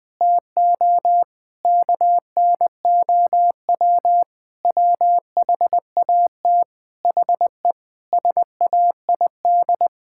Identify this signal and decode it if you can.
Morse code, heard as a keyed tone, about 20 wpm, 700 Hz. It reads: TO KNOW WHAT HE SAID
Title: Sentences from Top 1000 Words - Review Part 6 - 20wpm